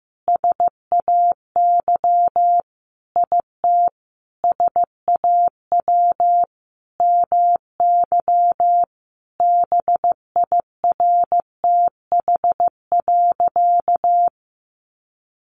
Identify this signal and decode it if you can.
Morse code, heard as a keyed tone, about 15 wpm, 700 Hz. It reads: SAY IT SAW MY BIRTH.
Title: Sentences from Top 800 Words 15wpm